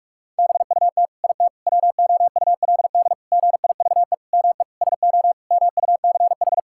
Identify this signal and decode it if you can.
Morse code, heard as a keyed tone, about 46 wpm, 700 Hz. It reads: BUT IT WOULD GIVE ME SO MUCH
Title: Sentences from Top 1600 Words 45wpm